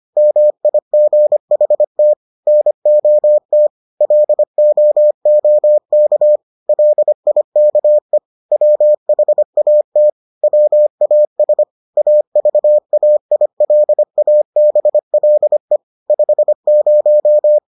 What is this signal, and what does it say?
Morse code, heard as a keyed tone, about 25 wpm, 600 Hz. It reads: MIGHT NOT LOOK LIKE WHAT WAS AVAILABLE 50